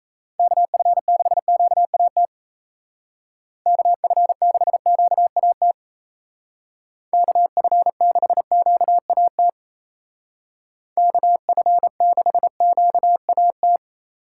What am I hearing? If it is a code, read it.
Morse code, heard as a keyed tone, about 37 wpm, 700 Hz. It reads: KF6QAT KF6QAT KF6QAT KF6QAT